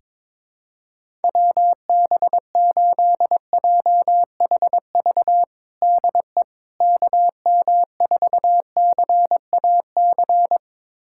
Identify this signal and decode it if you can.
Morse code, heard as a keyed tone, about 21 wpm, 700 Hz. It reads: WB8JHV DE KM4CAC
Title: USA Callsign DE Callsign 22wpm